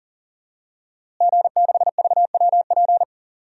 Morse code, heard as a keyed tone, about 40 wpm, 700 Hz. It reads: G6VWP
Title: DXCC-100 Call Signs - 5 Characters - Repeated 2x - 40wpm